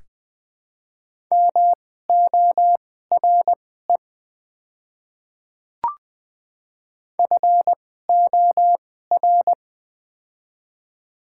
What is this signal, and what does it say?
Morse code, heard as a keyed tone, about 20 wpm, spaced as for 14 wpm, 700 Hz. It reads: MORE FOR
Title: 2x Character Spacing - Top 100 Words - 20wpm